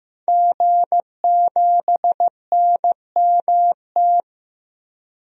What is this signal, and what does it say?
Morse code, heard as a keyed tone, about 15 wpm, 700 Hz. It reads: G7NMT
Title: DXCC-100 Call Signs - 5 Characters - Repeated 3x - 15wpm